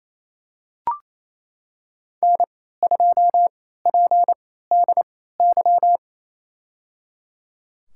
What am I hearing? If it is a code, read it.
Morse code, heard as a keyed tone, about 28 wpm, spaced as for 16 wpm, 700 Hz. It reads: N2PDY
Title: USA Call Signs - 3x Character Spacing - 28wpm